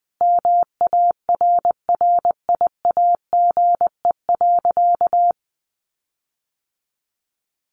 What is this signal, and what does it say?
Morse code, heard as a keyed tone, about 20 wpm, 700 Hz. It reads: MARRIAGE.
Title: Sentences from Top 1500 Words 20wpm